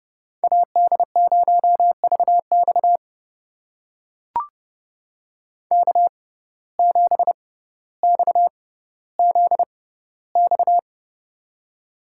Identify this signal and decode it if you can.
Morse code, heard as a keyed tone, about 31 wpm, spaced as for 15 wpm, 700 Hz. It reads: AD0VX K7XZX
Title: USA Call Signs - 5 Characters - 6x Character Spacing - 30wpm